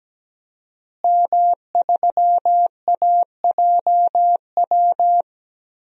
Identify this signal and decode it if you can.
Morse code, heard as a keyed tone, about 17 wpm, 700 Hz. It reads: M3AJW
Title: DXCC-100 Call Signs - 5 Characters - Repeated 2x - 17wpm